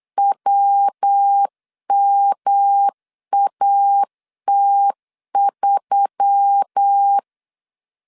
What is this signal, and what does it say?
Morse code, heard as a keyed tone, about 8 wpm, 800 Hz. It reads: WMAT3
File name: CW / DET / 1 6 1500C S B 8wpm 5142